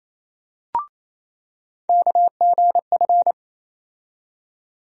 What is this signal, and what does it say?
Morse code, heard as a keyed tone, about 27 wpm, 700 Hz. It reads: KGF
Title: USA Call Sign Postfixes - 3 Letters 28wpm